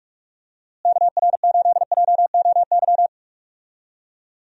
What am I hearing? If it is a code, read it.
Morse code, heard as a keyed tone, about 42 wpm, 700 Hz. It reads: KR8JOY
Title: USA Call Signs - 6 Characters - Repeated 3x - 45wpm